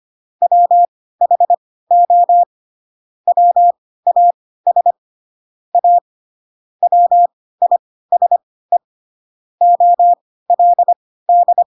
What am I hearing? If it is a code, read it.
Morse code, heard as a keyed tone, about 25 wpm, spaced as for 16 wpm, 700 Hz. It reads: WHO WAS A WISE OLD
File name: Pinocchio / Pinocchio ch4 10 25 ik1hge